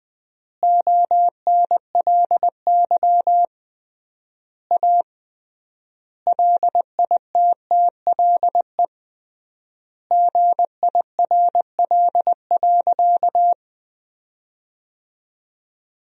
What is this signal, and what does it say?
Morse code, heard as a keyed tone, about 20 wpm, 700 Hz. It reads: ONLY A LITTLE GIRL.
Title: 3x Word-Spacing - Sentences from Top 400 Words - 20wpm